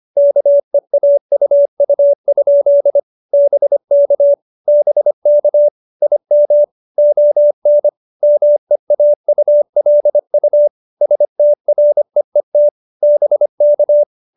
Morse code, heard as a keyed tone, about 25 wpm, 590 Hz. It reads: KEAUU? BK BK IM ON MEAULU STREET BK